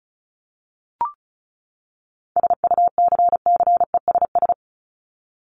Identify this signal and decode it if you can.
Morse code, heard as a keyed tone, about 34 wpm, 700 Hz. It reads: SUCCESS
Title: Top 751-1000 Words 35wpm